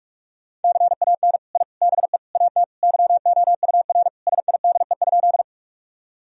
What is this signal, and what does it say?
Morse code, heard as a keyed tone, about 46 wpm, 700 Hz. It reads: CAN I BE AT YOUR SIDE?
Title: Sentences from Top 1000 Words - Review Part 1 - 45wpm